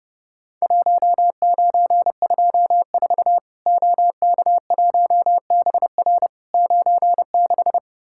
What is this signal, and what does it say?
Morse code, heard as a keyed tone, about 30 wpm, 700 Hz. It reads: 1924 OK1BR 96
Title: Worked All Europe DX Contest - QTC Exchange - 30wpm